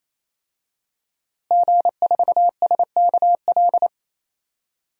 Morse code, heard as a keyed tone, about 28 wpm, 700 Hz. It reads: G4SKL
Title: DXCC-100 Call Signs - 5 Characters - Repeated 3x - 28wpm